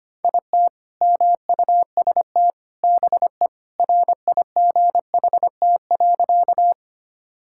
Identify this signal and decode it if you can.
Morse code, heard as a keyed tone, about 25 wpm, 700 Hz. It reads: IT MUST BE RIGHT.
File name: practice / 25wpm / Sentences with 2-Word Phrases - 25wpm